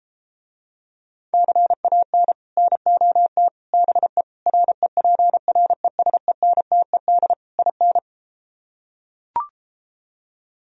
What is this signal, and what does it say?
Morse code, heard as a keyed tone, about 33 wpm, 700 Hz. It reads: CAN NOT BE REPRESENTED IN